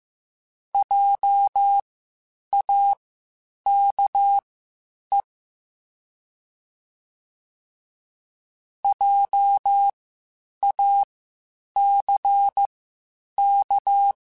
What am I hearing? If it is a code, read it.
Morse code, heard as a keyed tone, about 15 wpm, spaced as for 8 wpm, 800 Hz. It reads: JAKE JACK
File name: Spaced Alphas / Spaced Alphas mp3s / Js sp @13